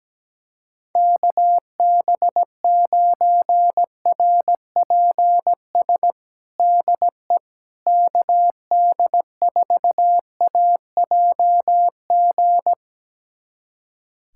Morse code, heard as a keyed tone, about 17 wpm, 700 Hz. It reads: KB9RPS DE KD4AJG
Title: USA Callsign DE Callsign - 6 Characters - 17wpm